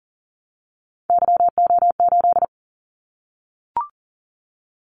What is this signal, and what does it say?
Morse code, heard as a keyed tone, about 38 wpm, 700 Hz. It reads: YO8
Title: Callsign Prefixes for Top 25 DXCC - 40wpm